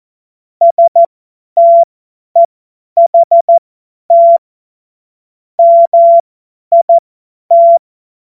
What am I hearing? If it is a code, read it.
Morse code, heard as a keyed tone, about 14 wpm, spaced as for 10 wpm, 680 Hz. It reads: STEHT MIT
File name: Morsekurs / Witze / Witz 15. August 2023 50 F70